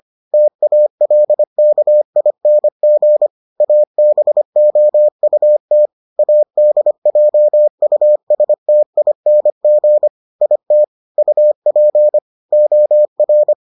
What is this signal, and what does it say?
Morse code, heard as a keyed tone, about 25 wpm, 600 Hz. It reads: TALKING ABOUT ADJUSTING IT UP OR